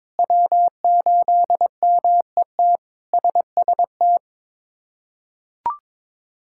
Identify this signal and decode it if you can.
Morse code, heard as a keyed tone, about 22 wpm, 700 Hz. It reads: W8MET SST
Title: K1USN SST QSO Practice - 22wpm